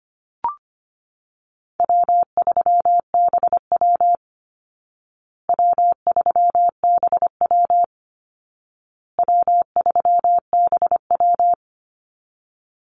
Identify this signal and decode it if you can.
Morse code, heard as a keyed tone, about 25 wpm, 700 Hz. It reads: W3BW W3BW W3BW
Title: Easy Call Signs Repeated 3x - 25wpm